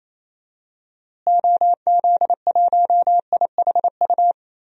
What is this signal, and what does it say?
Morse code, heard as a keyed tone, about 28 wpm, 700 Hz. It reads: OZ1IHU
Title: DXCC-100 Call Signs - 6 Characters - Repeated 2x - 28wpm